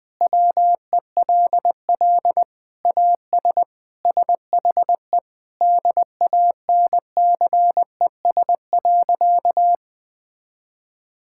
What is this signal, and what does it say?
Morse code, heard as a keyed tone, about 20 wpm, 700 Hz. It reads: WELL AS SHE DANCES.
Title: Sentences with 3-Word Phrases - 20wpm